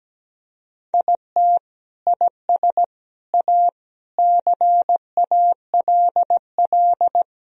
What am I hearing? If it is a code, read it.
Morse code, heard as a keyed tone, about 17 wpm, 700 Hz. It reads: IT IS A CALL
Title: Sentences from Top 1000 Words - Review Part 4 - 17wpm